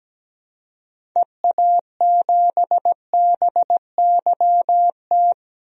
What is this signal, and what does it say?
Morse code, heard as a keyed tone, about 17 wpm, 700 Hz. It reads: EA7BYT
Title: DXCC-100 Call Signs - 6 Characters - 17wpm